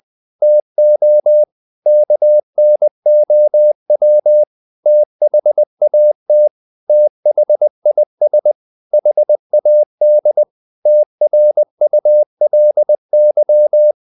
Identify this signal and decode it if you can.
Morse code, heard as a keyed tone, about 20 wpm, 600 Hz. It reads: TO KNOW THAT THIS HAD TRULY
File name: the old man and the sea part 2 20wpm